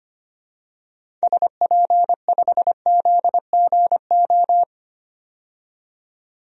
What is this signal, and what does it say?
Morse code, heard as a keyed tone, about 25 wpm, 700 Hz. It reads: SP5ZGO
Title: DXCC-100 Call Signs - 6 Characters - 25wpm